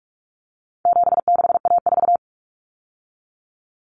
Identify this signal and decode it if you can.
Morse code, heard as a keyed tone, about 44 wpm, 700 Hz. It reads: 76A4
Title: Common Cut Numbers - 4-Digit 45wpm